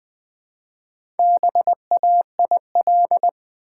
Morse code, heard as a keyed tone, about 20 wpm, 700 Hz. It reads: BAIL
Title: Build A Word - 2 Letters At A Time - 4-Letter Words - 20wpm